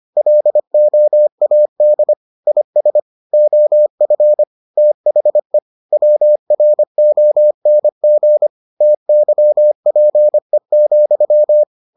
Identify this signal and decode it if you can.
Morse code, heard as a keyed tone, about 25 wpm, 600 Hz. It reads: LOAD IS OF THE WRONG TYPE,